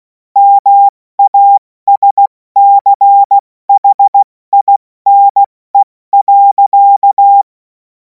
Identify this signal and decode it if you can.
Morse code, heard as a keyed tone, about 16 wpm, 800 Hz. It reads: MASCHINE.
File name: Morsekurs / Witze / Witz 31. März 2026 70 F80